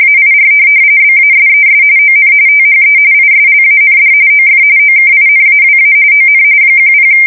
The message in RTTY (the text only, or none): AIR BASES ON THE ISLAND OF OKINAWA ON 10